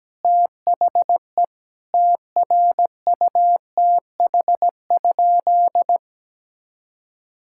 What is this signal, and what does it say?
Morse code, heard as a keyed tone, about 17 wpm, 700 Hz. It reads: THE TRUTH?
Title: Sentences from Top 1600 Words 17wpm